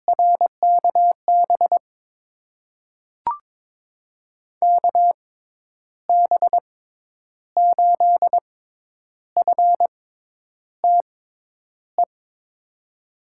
Morse code, heard as a keyed tone, about 22 wpm, spaced as for 9 wpm, 700 Hz. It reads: RKB KB8FTE
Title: USA Call Signs - 6 Characters - 6x Character Spacing - 22wpm